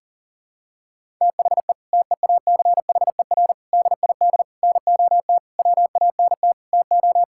Morse code, heard as a keyed tone, about 40 wpm, 700 Hz. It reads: THE TEACHER DID NOT WANT TO
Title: Sentences from Top 1000 Words - Review Part 6 - 40wpm